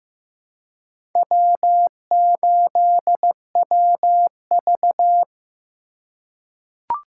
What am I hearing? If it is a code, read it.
Morse code, heard as a keyed tone, about 15 wpm, 700 Hz. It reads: W8WV
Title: USA Call Signs - 4 Characters - 2x Character Spacing - 15wpm